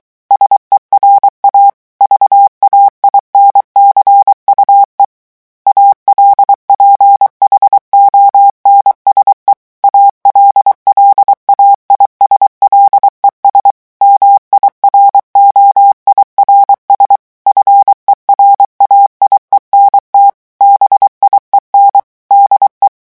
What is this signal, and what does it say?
Morse code, heard as a keyed tone, about 23 wpm, 800 Hz. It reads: SERA VAINCUE ALPHONSE ALLAIS LES MIROIRS FERAIENT BIEN DE
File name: morse mp3 / revision9 citations humoristiques vitesse 22 mots minute